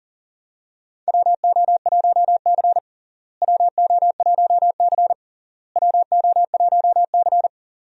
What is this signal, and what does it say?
Morse code, heard as a keyed tone, about 40 wpm, 700 Hz. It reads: WO1C WO1C WO1C
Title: USA Call Signs - 4 Characters - Repeated 3x - 40wpm